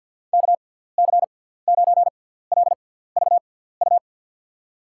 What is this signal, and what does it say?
Morse code, heard as a keyed tone, about 54 wpm, spaced as for 17 wpm, 700 Hz. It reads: KC9LVU
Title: USA Call Signs - 6 Characters - 6x Character Spacing - 50wpm